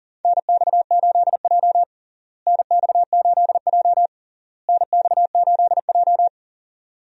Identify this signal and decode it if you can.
Morse code, heard as a keyed tone, about 40 wpm, 700 Hz. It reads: NX8J NX8J NX8J
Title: USA Call Signs - 4 Characters - Repeated 3x - 40wpm